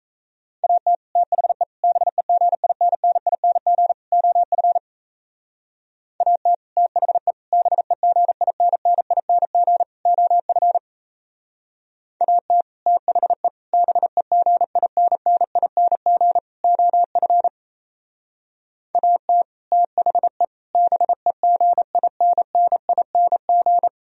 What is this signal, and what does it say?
Morse code, heard as a keyed tone, about 35 wpm, 700 Hz. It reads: AT THE BEGINNING OF AT THE BEGINNING OF AT THE BEGINNING OF AT THE BEGINNING
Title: Speed-Racing - Sets of 4 Words 28wpm